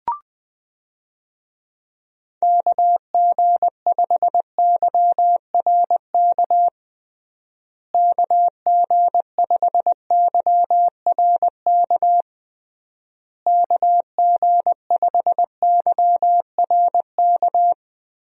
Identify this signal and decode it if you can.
Morse code, heard as a keyed tone, about 20 wpm, 700 Hz. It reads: KG5YRK KG5YRK KG5YRK
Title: USA Call Signs - 6 Characters - Repeated 3x - 20wpm